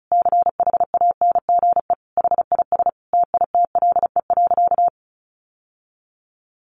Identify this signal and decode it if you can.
Morse code, heard as a keyed tone, about 34 wpm, 700 Hz. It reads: CHANGE HIS TITLE.
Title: Sentences from Top 1200 Words 35wpm